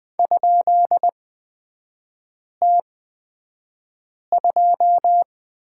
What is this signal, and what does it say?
Morse code, heard as a keyed tone, about 20 wpm, spaced as for 5 wpm, 700 Hz. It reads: ?T2
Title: 250 Lesson 40e - BK - Three Characters 20wpm